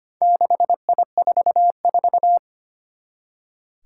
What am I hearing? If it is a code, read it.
Morse code, heard as a keyed tone, about 25 wpm, 700 Hz. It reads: 6I44
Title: Four Challenging Letter-Number Combinations 25wpm